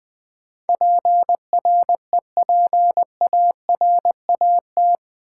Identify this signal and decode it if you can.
Morse code, heard as a keyed tone, about 20 wpm, 700 Hz. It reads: PREPARAT
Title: Build A Word - 4 Letters At A Time - 11-Letter Words - 20wpm